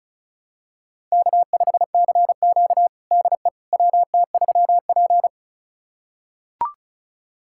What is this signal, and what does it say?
Morse code, heard as a keyed tone, about 36 wpm, 700 Hz. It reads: K5CQ DE WT3P
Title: USA Callsign DE Callsign - 4 Characters - 35wpm